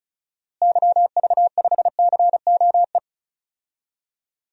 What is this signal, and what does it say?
Morse code, heard as a keyed tone, about 36 wpm, 700 Hz. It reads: YV5COE
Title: DXCC-100 Call Signs - 6 Characters - Repeated 3x - 35wpm